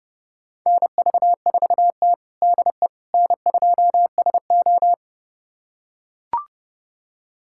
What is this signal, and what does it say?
Morse code, heard as a keyed tone, about 32 wpm, 700 Hz. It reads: NV4T DE N2SO
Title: USA Callsign DE Callsign - 4 Characters - 30wpm